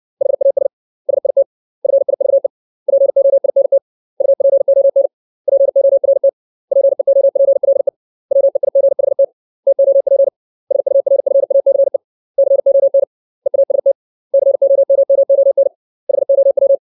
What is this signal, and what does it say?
Morse code, heard as a keyed tone, about 60 wpm, 550 Hz. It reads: HAS SIT FIVE POINT UPON PORT PEOPLE WEIGHT TOP SURFACE CON EAST COMMON HOW
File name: 632 words / 632 Words 60WPM